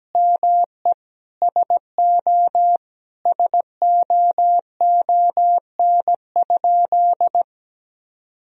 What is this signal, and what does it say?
Morse code, heard as a keyed tone, about 17 wpm, 700 Hz. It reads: ME SO SOON?
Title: Sentences from Top 400 Words and Abbreviations 17wpm